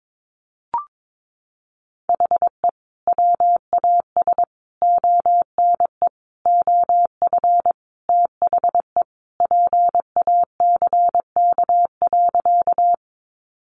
Morse code, heard as a keyed tone, about 22 wpm, 700 Hz. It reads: HE WAS ONE OF THE PACK.
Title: Sentences from Top 1300 Words 22wpm